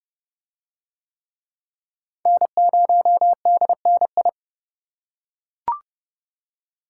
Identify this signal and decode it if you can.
Morse code, heard as a keyed tone, about 28 wpm, 700 Hz. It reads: N0DNI